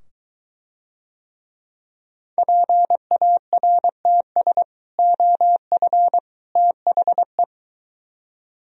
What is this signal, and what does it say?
Morse code, heard as a keyed tone, about 23 wpm, 700 Hz. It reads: PARTS OF THE